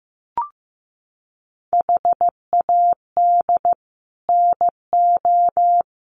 Morse code, heard as a keyed tone, about 14 wpm, 700 Hz. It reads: HAD NO
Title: Sets of 5 Words 15wpm